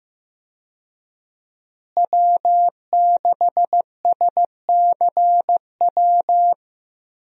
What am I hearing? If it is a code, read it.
Morse code, heard as a keyed tone, about 15 wpm, 700 Hz. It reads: W6SCW
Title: USA Call Signs - 5 Characters - Repeated 2x - 15wpm